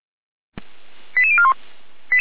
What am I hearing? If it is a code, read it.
Five-tone selective call (ZVEI), heard as quick runs of five tones, tones of about 70 ms. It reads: 88041